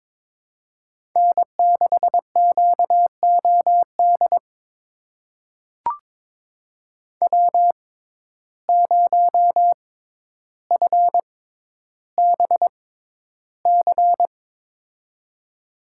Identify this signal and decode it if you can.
Morse code, heard as a keyed tone, about 22 wpm, spaced as for 11 wpm, 700 Hz. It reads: N6QOD W0FBC
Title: USA Call Signs - 5 Characters - 6x Character Spacing - 22wpm